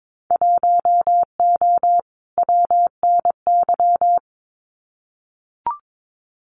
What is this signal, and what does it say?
Morse code, heard as a keyed tone, about 22 wpm, 700 Hz. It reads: 1O WNY
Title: Winter Field Day Exchanges Encore 22wpm